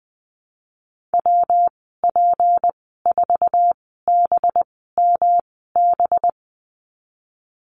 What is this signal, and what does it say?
Morse code, heard as a keyed tone, about 20 wpm, spaced as for 15 wpm, 700 Hz. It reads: WP4BMB
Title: 186 Lesson 30g - Letter b - Call Signs 20wpm